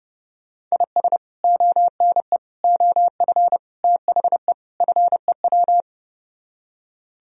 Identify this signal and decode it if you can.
Morse code, heard as a keyed tone, about 36 wpm, 700 Hz. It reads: IS ONE OF THE FEW
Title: Speed-Racing - Sets of 5 Words 30wpm